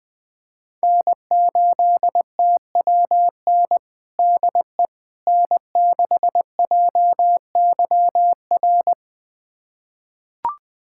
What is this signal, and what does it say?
Morse code, heard as a keyed tone, about 20 wpm, 700 Hz. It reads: N8TWN DE N6JYR
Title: USA Callsign DE Callsign 20wpm